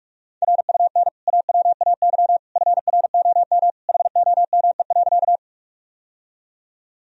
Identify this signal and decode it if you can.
Morse code, heard as a keyed tone, about 46 wpm, 700 Hz. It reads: RUN AWAY FROM HOME.
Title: Sentences from Top 1000 Words - Review Part 1 - 45wpm